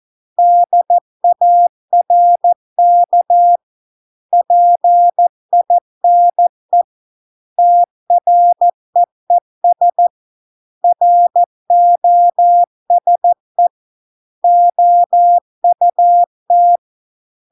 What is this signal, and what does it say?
Morse code, heard as a keyed tone, about 14 wpm, 700 Hz. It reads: DARK PINE TREES ROSE OUT